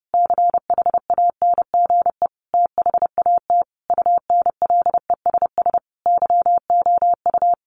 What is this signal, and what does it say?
Morse code, heard as a keyed tone, about 30 wpm, 700 Hz. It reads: CHANGE THAT UNLESS YOU
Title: A Sound Of Thunder 30wpm